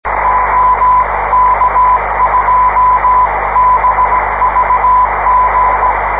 Morse code, heard as a keyed tone, about 17 wpm, 1000 Hz, with radio noise. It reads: OK2BVG